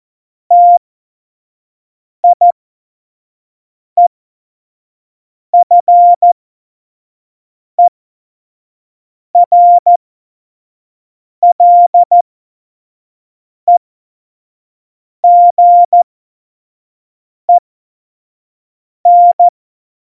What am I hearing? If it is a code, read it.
Morse code, heard as a keyed tone, about 14 wpm, spaced as for 5 wpm, 700 Hz. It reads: TIEFERLEGEN
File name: Morsekurs / Witze / Witz 07 Juli 2020, 25 F70